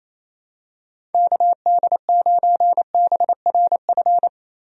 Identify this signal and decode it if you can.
Morse code, heard as a keyed tone, about 28 wpm, 700 Hz. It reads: KD9BRF
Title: USA Call Signs - 6 Characters - Repeated 2x - 28wpm